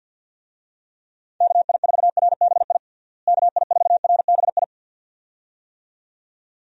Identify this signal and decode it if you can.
Morse code, heard as a keyed tone, about 52 wpm, 700 Hz. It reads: KI4RBI KI4RBI
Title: USA Call Signs - 6 Characters - Repeated 2x - 50wpm